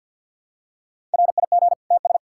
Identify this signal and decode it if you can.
Morse code, heard as a keyed tone, about 46 wpm, 700 Hz. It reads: RIG TS
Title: Practice QSO - Exchange 2 - Info - 50wpm